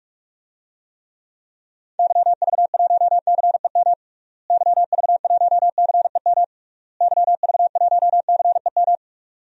Morse code, heard as a keyed tone, about 46 wpm, 700 Hz. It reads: YV1CEM YV1CEM YV1CEM
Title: DXCC-100 Call Signs - 6 Characters - Repeated 3x - 45wpm